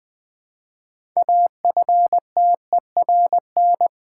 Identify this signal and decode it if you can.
Morse code, heard as a keyed tone, about 20 wpm, 700 Hz. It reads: AFTERN
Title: Build A Word - 3 Letters At A Time - 9-Letter Words - 20wpm